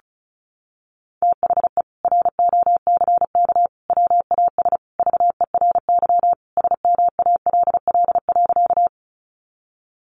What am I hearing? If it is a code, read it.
Morse code, heard as a keyed tone, about 34 wpm, 700 Hz. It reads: THE ROCK WAS VERY SMALL.
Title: Sentences from Top 400 Words 35wpm